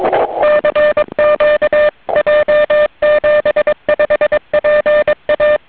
Morse code, heard as a keyed tone, about 22 wpm, 590 Hz, with noise. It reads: CQ J75PA
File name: j7 vp2m 2002 / Voice